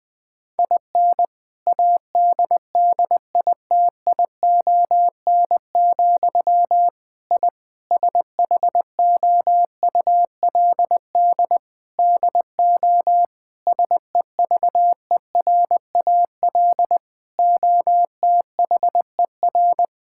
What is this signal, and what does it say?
Morse code, heard as a keyed tone, about 20 wpm, 700 Hz. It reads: IN ADDITION, I SHOULD DO SEVERAL OTHER
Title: Sentences from Top 1000 Words - Review Part 6 - 20wpm